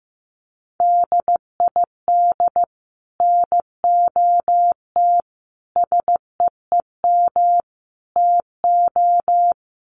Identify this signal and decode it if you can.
Morse code, heard as a keyed tone, about 15 wpm, 700 Hz. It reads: DID NOT SEEM TO
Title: Sets of 4 Words 15wpm